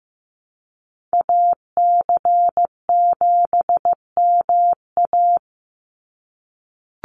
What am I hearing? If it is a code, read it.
Morse code, heard as a keyed tone, about 15 wpm, 700 Hz. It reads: AC7MA
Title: NA SOTA Activator Callsigns 15wpm